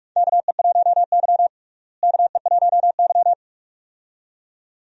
Morse code, heard as a keyed tone, about 46 wpm, 700 Hz. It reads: KE1Y KE1Y
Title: USA Call Signs - 4 Characters - Repeated 2x - 45wpm